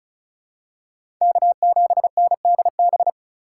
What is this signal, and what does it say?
Morse code, heard as a keyed tone, about 36 wpm, 700 Hz. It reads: K7NDB
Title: USA Call Signs - 3x Character Spacing - 35wpm